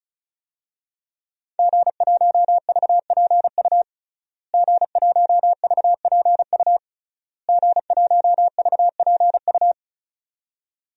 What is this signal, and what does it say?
Morse code, heard as a keyed tone, about 36 wpm, 700 Hz. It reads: G1VPU G1VPU G1VPU
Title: DXCC-100 Call Signs - 5 Characters - Repeated 3x - 35wpm